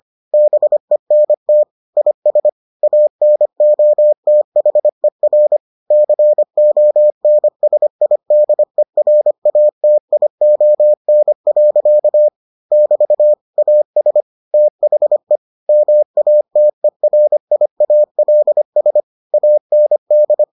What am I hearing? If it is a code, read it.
Morse code, heard as a keyed tone, about 25 wpm, 600 Hz. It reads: BENT IS ANOTHER CONSIDERATION. <BT> AS THE MATERIALS AND